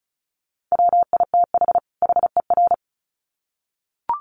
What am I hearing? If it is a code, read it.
Morse code, heard as a keyed tone, about 34 wpm, 700 Hz. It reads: WITH HER
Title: Sets of 2 Words 35wpm